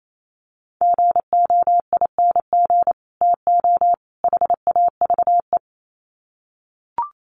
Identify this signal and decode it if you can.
Morse code, heard as a keyed tone, about 28 wpm, 700 Hz. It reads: GOING TO HAVE
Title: Sets of 3 Words 28wpm